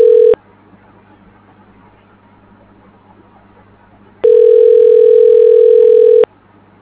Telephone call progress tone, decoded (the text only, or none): ringback tone